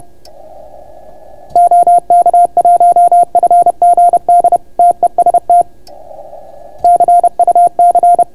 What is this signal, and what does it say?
Morse code, heard as a keyed tone, about 31 wpm, 700 Hz. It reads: OK1FGD TEST CUC